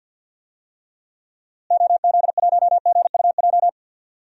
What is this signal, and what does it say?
Morse code, heard as a keyed tone, about 52 wpm, 700 Hz. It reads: OZ1GUJ
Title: DXCC-100 Call Signs - 6 Characters - Repeated 3x - 50wpm